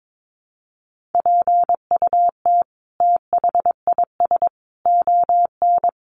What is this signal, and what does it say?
Morse code, heard as a keyed tone, about 22 wpm, 700 Hz. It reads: PUT THIS ON
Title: Sentences from Top 1500 Words 22wpm